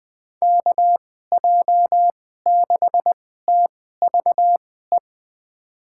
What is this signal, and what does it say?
Morse code, heard as a keyed tone, about 20 wpm, spaced as for 14 wpm, 700 Hz. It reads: KJ6TVE
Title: USA Call Signs - 6 Characters - 2x Character Spacing - 20wpm